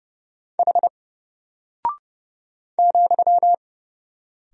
Morse code, heard as a keyed tone, about 30 wpm, 700 Hz. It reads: H ,
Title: Single Character (Letter:Number:Punctuation:Prosign) Alt 30wpm